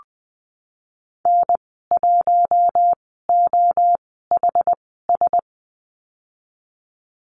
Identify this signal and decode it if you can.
Morse code, heard as a keyed tone, about 20 wpm, spaced as for 15 wpm, 700 Hz. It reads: N1OHS
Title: 076 Lesson 13g - Letter L - Call Signs 20wpm